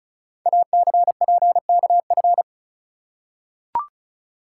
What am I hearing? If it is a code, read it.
Morse code, heard as a keyed tone, about 36 wpm, 700 Hz. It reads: ACPKF
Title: Code Groups - 5 Letter:Numbers - 35wpm